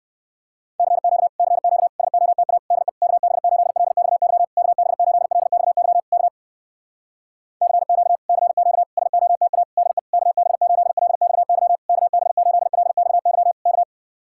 Morse code, heard as a keyed tone, about 59 wpm, 700 Hz. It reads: CQ CQ SOTA DE KB8LXY KB8LXY K CQ CQ SOTA DE KB8LXY KB8LXY K